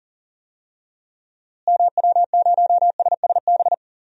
Morse code, heard as a keyed tone, about 40 wpm, 700 Hz. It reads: MW0SSB
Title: DXCC-100 Call Signs - 6 Characters - Repeated 3x - 40wpm